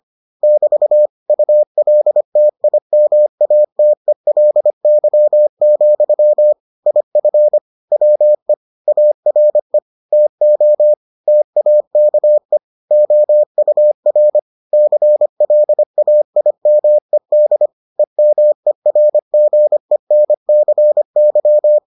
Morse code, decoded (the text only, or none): <BT> ULTIMATELY, IF WE ARE TO TAKE OUR CLAIMED EMERGENCY